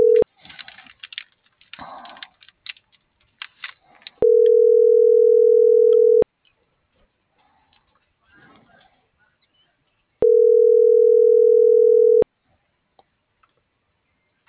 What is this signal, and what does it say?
Ringback tone (North America), 440 + 480 Hz.